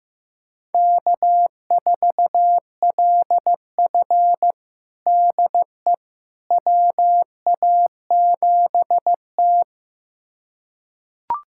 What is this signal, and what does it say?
Morse code, heard as a keyed tone, about 16 wpm, 700 Hz. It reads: K4LF DE WA7T
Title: USA Callsign DE Callsign - 4 Characters - 15wpm